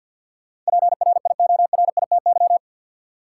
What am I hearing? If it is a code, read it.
Morse code, heard as a keyed tone, about 47 wpm, 700 Hz. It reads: PRIORITY
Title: Sentences from Top 1600 Words - New Word Drill - 50wpm